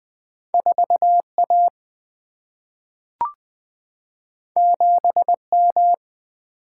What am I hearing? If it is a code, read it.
Morse code, heard as a keyed tone, about 20 wpm, 700 Hz. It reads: 4A 7M
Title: Two Letter-Number Combinations - 20wpm